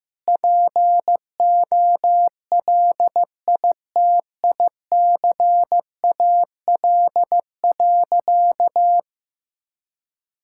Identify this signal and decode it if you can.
Morse code, heard as a keyed tone, about 15 wpm, 700 Hz. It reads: POLITICAL.
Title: Sentences from Top 1000 Words - Review Part 5 - 15wpm